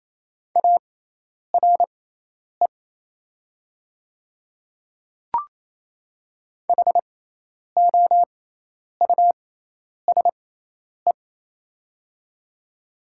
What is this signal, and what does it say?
Morse code, heard as a keyed tone, about 28 wpm, spaced as for 10 wpm, 700 Hz. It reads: ARE HOUSE
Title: 6x Character Spacing - Top 300 Words - 28wpm